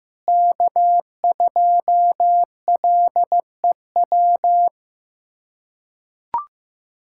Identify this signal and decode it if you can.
Morse code, heard as a keyed tone, about 15 wpm, 700 Hz. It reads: K2LEW
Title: USA Callsign DE Callsign - 5 Characters - 15wpm